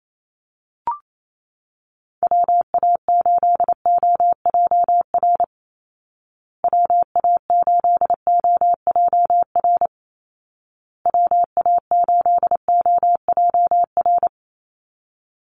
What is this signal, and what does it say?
Morse code, heard as a keyed tone, about 28 wpm, 700 Hz. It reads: WA8OJR WA8OJR WA8OJR
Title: Call Signs Repeated 2x - 28wpm